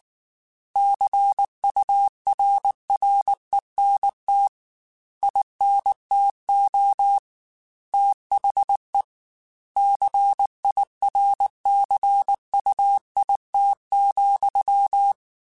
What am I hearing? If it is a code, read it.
Morse code, heard as a keyed tone, about 19 wpm, 800 Hz. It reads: CURRENT INTO THE CIRCUIT,
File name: Technical Subjects / Technical Subjects-mp3s / XC @18